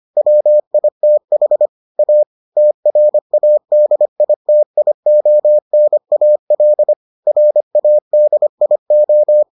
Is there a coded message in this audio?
Yes: Morse code from 0.2 to 9.4 s, heard as a keyed tone, about 25 wpm, 600 Hz. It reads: WITH A TRADITIONAL RADIO